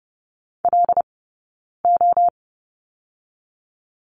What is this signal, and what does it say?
Morse code, heard as a keyed tone, about 30 wpm, 700 Hz. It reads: L O
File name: Lesson 19d - Letter w - Two Characters 30wpm